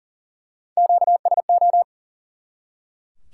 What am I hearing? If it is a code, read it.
Morse code, heard as a keyed tone, about 42 wpm, 700 Hz. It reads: QSO
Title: Top 180 QSO Elements 40wpm